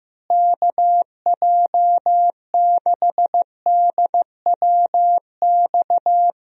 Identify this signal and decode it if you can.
Morse code, heard as a keyed tone, about 15 wpm, 700 Hz. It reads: KJ6DWX
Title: USA Call Signs - 6 Characters - Repeated 3x - 15wpm